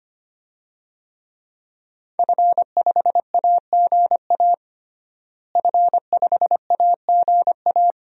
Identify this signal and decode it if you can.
Morse code, heard as a keyed tone, about 25 wpm, 700 Hz. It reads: F5AGA F5AGA
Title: DXCC-100 Call Signs - 5 Characters - Repeated 2x - 25wpm